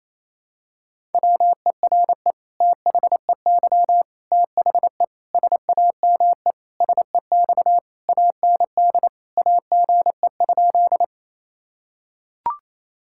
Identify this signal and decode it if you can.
Morse code, heard as a keyed tone, about 27 wpm, 700 Hz. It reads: WERE THEY THE SAME SEX AND AGE?
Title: Sentences from Top 1600 Words 28wpm